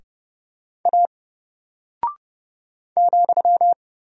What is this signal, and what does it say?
Morse code, heard as a keyed tone, about 30 wpm, 700 Hz. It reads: A ,
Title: Single Character (Letter:Number:Punctuation:Prosign) Alt 30wpm